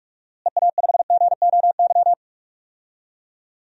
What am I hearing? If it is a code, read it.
Morse code, heard as a keyed tone, about 44 wpm, 700 Hz. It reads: EA5GOY
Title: DXCC-100 Call Signs - 6 Characters - Repeated 3x - 45wpm